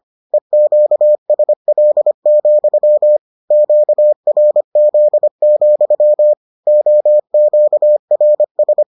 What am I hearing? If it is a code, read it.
Morse code, heard as a keyed tone, about 25 wpm, 600 Hz. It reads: EQSL, QRZ, OQRS